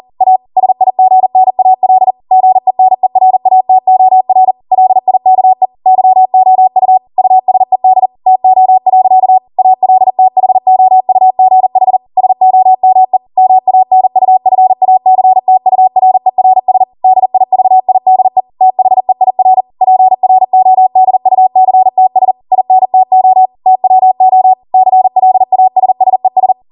Morse code, heard as a keyed tone, about 40 wpm, 750 Hz. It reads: A SIGNAL GENERATOR LIKE YOU USED TO. ALTHOUGH SOME MANUFACTURERS DIVIDE THEIR PRODUCTS INTO TWO CLASSES